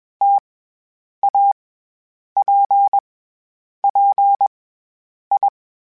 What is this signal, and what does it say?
Morse code, heard as a keyed tone, about 21 wpm, 800 Hz. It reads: T A P P I
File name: Difficult Combos / Difficult Combos-mp3s / T AOEI @20